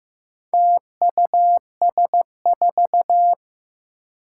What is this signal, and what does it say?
Morse code, heard as a keyed tone, about 15 wpm, 700 Hz. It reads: TUS4
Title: Four Challenging Letter-Number Combinations 15wpm